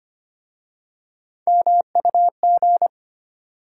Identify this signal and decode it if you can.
Morse code, heard as a keyed tone, about 25 wpm, 690 Hz. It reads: MUG
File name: practice / 25wpm / Easy Words with Hints - 25wpm